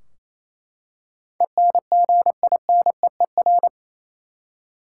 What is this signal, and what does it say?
Morse code, heard as a keyed tone, about 28 wpm, 700 Hz. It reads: ENGINEER
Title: Practice QSO - Exchange 3 - More Info - 28wpm